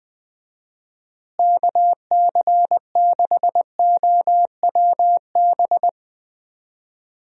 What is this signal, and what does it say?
Morse code, heard as a keyed tone, about 20 wpm, 700 Hz. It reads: KC6OWB
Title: USA Call Signs - 6 Characters - Repeated 3x - 20wpm